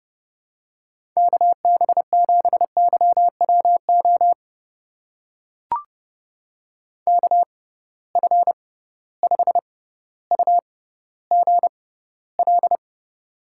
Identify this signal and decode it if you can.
Morse code, heard as a keyed tone, about 30 wpm, spaced as for 15 wpm, 700 Hz. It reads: KB7YWO KF5UGL
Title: USA Call Signs - 6 Characters - 6x Character Spacing - 30wpm